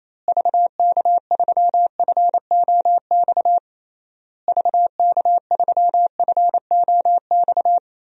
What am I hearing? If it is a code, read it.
Morse code, heard as a keyed tone, about 28 wpm, 700 Hz. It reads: VK3FOX VK3FOX
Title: DXCC-100 Call Signs - 6 Characters - Repeated 3x - 28wpm